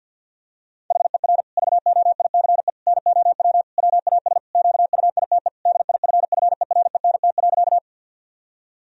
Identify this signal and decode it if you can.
Morse code, heard as a keyed tone, about 50 wpm, 700 Hz. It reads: HER VOICE NOW WAS QUITE DIFFERENT.
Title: Sentences from Top 1000 Words - Review Part 3 - 50wpm